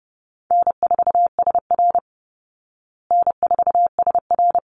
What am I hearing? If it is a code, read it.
Morse code, heard as a keyed tone, about 28 wpm, 700 Hz. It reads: N4SR N4SR
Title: Call Signs Repeated 3x - 30wpm